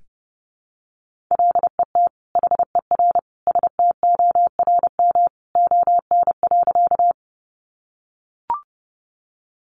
Morse code, heard as a keyed tone, about 30 wpm, 700 Hz. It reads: LET HER STORM ON.
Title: Sentences from Top 1500 Words 30wpm